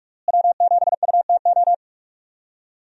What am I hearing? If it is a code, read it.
Morse code, heard as a keyed tone, about 42 wpm, 700 Hz. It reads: W7UTO
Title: USA Call Signs - 5 Characters - Repeated 3x - 45wpm